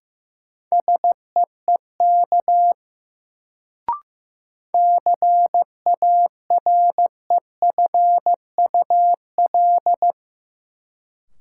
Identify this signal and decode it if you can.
Morse code, heard as a keyed tone, about 15 wpm, 700 Hz. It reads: SEEK CAREFUL
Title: Sentences from Top 1600 Words - New Word Drill Alt - 15wpm